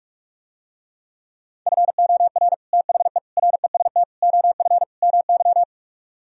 Morse code, heard as a keyed tone, about 48 wpm, 700 Hz. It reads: FOR THE REST OF MY